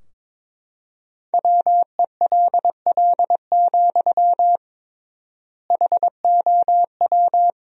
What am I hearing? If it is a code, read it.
Morse code, heard as a keyed tone, about 22 wpm, 700 Hz. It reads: WELL, HOW
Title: 3x Word-Spacing - Sentences from Top 200 Words - 22wpm